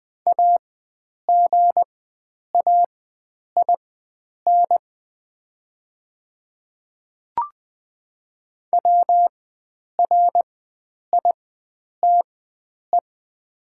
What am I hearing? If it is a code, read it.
Morse code, heard as a keyed tone, about 20 wpm, spaced as for 9 wpm, 700 Hz. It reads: AGAIN WRITE
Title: 4x Character Spacing - Top 500 Words - 20wpm